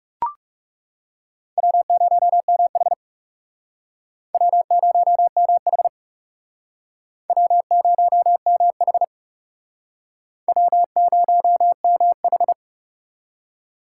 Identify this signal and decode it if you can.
Morse code, heard as a keyed tone, about 40 wpm, 700 Hz. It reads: W0MH W0MH W0MH W0MH